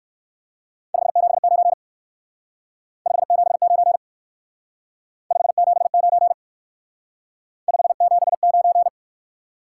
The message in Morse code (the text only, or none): 579 579 579 579